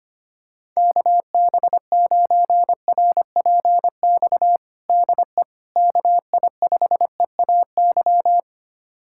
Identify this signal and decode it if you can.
Morse code, heard as a keyed tone, about 25 wpm, 700 Hz. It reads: KB9RPX DE KI5EAY
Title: USA Callsign DE Callsign - 6 Characters - 25wpm